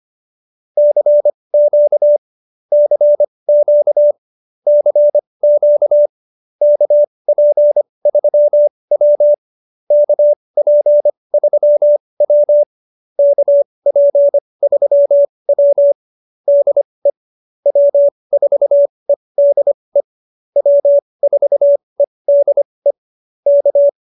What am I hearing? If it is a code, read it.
Morse code, heard as a keyed tone, about 25 wpm, spaced as for 20 wpm, 590 Hz. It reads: CQ CQ CQ KP3W KP3W KP3W DE W4EDE W4EDE K
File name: QSO317 20